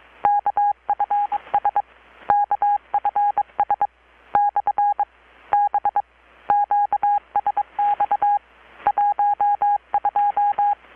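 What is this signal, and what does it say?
Morse code, heard as a keyed tone, about 22 wpm, spaced as for 18 wpm, 820 Hz. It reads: KFS KFS/B QSX 12
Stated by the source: Morse from KPH's Night of Nights XX maritime radiotelegraphy event, received in Chiapas, Mexico about 3,800 km away on a Kenwood R-600 with a grounded 10 m long-wire antenna.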